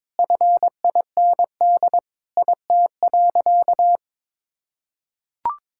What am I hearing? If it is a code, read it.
Morse code, heard as a keyed tone, about 22 wpm, 700 Hz. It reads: FIND IT.
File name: practice / 22wpm / Sentences with 3-Word Phrases - 22wpm